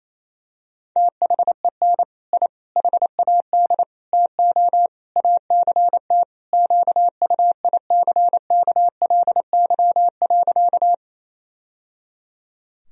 Morse code, heard as a keyed tone, about 28 wpm, 700 Hz. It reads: THEN I HAD TO ACT QUICKLY.
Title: Sentences from Top 1000 Words - Review Part 1 - 28wpm